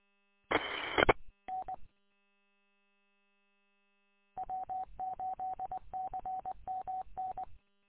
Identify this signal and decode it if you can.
Morse code, heard as a keyed tone, about 20 wpm, 750 Hz. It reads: E I N W8CMN
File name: w8cmn- -combined